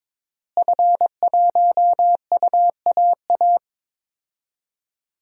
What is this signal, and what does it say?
Morse code, heard as a keyed tone, about 22 wpm, 700 Hz. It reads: F1UAA
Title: DXCC-100 Call Signs - 5 Characters - Repeated 3x - 22wpm